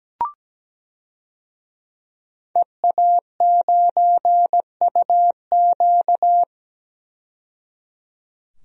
Morse code, heard as a keyed tone, about 17 wpm, 700 Hz. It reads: EA9UQ
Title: DXCC-100 Call Signs - 5 Characters - 17wpm